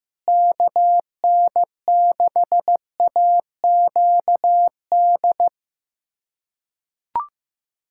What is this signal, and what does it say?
Morse code, heard as a keyed tone, about 15 wpm, 700 Hz. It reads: KN6AQD
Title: USA Callsign DE Callsign - 6 Characters - 15wpm